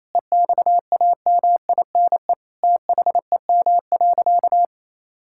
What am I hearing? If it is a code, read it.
Morse code, heard as a keyed tone, about 28 wpm, 700 Hz. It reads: EXAMINE THEM.
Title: Sentences from Top 1600 Words 28wpm